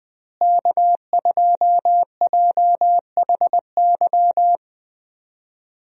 Morse code, heard as a keyed tone, about 20 wpm, 700 Hz. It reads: K2JHY
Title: USA Call Signs - 5 Characters - Repeated 2x - 20wpm